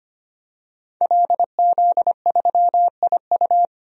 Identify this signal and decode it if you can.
Morse code, heard as a keyed tone, about 25 wpm, 700 Hz. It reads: LZ3IU
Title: DXCC-100 Call Signs - 5 Characters - 25wpm